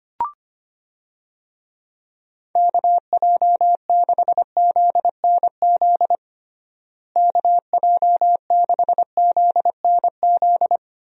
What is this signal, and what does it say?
Morse code, heard as a keyed tone, about 25 wpm, 700 Hz. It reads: KJ6ZNZ KJ6ZNZ
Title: USA Call Signs - 6 Characters - Repeated 3x - 25wpm